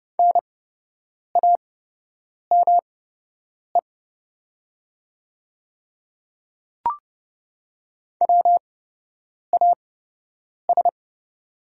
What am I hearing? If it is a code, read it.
Morse code, heard as a keyed tone, about 30 wpm, spaced as for 8 wpm, 700 Hz. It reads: NAME WAS
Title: 8x Character Spacing - Top 200 Words - 30wpm